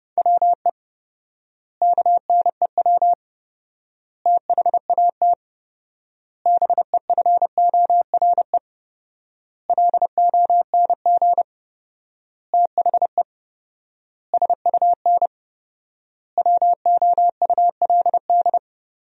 Morse code, heard as a keyed tone, about 32 wpm, 700 Hz. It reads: WE KNEW THAT BEFORE LONG THE SUN WOULD
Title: 4x Word-Spacing - Sentences from Top 400 Words - 30wpm